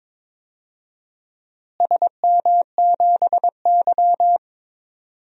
Morse code, heard as a keyed tone, about 22 wpm, 700 Hz. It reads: SM7Y
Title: DXCC-100 Call Signs - 4 Characters - Repeated 3x - 22wpm